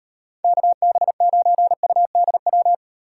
Morse code, heard as a keyed tone, about 38 wpm, 700 Hz. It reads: KB9UDW